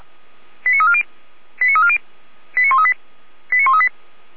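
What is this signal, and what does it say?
Five-tone selective call (ZVEI), heard as quick runs of five tones, tones of about 70 ms. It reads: 89340 89340 89138 89138
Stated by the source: ZVEI five-tone call received on SDR, from the Feuerwehr Orlamünde audio archive.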